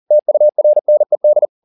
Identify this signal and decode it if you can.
Morse code, heard as a keyed tone, about 40 wpm, 600 Hz. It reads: TURNED